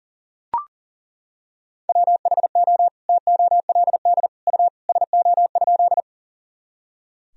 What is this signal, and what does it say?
Morse code, heard as a keyed tone, about 37 wpm, 700 Hz. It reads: WHO TOLD U SO?
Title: Sentences from Top 400 Words and Abbreviations 40wpm